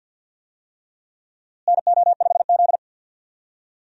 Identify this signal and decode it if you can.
Morse code, heard as a keyed tone, about 52 wpm, 700 Hz. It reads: NO5Z
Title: USA Call Signs - 4 Characters - Repeated 3x - 50wpm